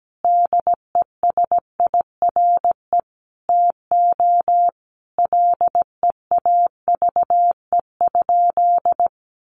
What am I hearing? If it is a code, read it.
Morse code, heard as a keyed tone, about 17 wpm, 700 Hz. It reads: DESIRE TO LEAVE?
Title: Sentences from Top 1100 Words 17wpm